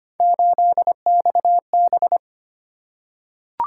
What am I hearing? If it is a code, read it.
Morse code, heard as a keyed tone, about 25 wpm, 700 Hz. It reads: 8XB
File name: Three Letter-Number Combinations - 25wpm